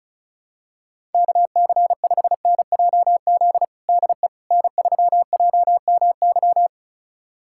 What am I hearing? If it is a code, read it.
Morse code, heard as a keyed tone, about 34 wpm, 700 Hz. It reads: KC5NJZ DE N3JMY
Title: USA Callsign DE Callsign 35wpm